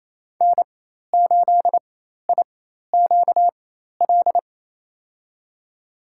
Morse code, heard as a keyed tone, about 28 wpm, spaced as for 13 wpm, 700 Hz. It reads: N8IQL
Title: USA Call Signs - 4x Character Spacing - 28wpm